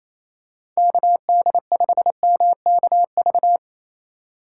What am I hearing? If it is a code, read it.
Morse code, heard as a keyed tone, about 28 wpm, 700 Hz. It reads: KD5MKV
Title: USA Call Signs - 6 Characters - 3x Character Spacing - 28wpm